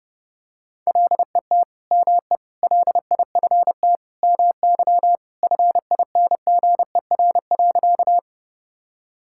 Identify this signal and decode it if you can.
Morse code, heard as a keyed tone, about 32 wpm, 700 Hz. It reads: LET ME LIFT MY FINGER.
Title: Sentences from Top 1600 Words 30wpm